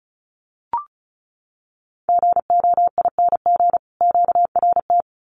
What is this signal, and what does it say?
Morse code, heard as a keyed tone, about 34 wpm, 700 Hz. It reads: GOING QRT
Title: Common QSO Phrases 35wpm